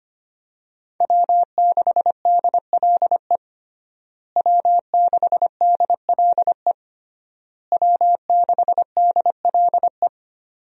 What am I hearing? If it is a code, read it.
Morse code, heard as a keyed tone, about 25 wpm, 700 Hz. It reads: W6DLE W6DLE W6DLE
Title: USA Call Signs - 5 Characters - Repeated 3x - 25wpm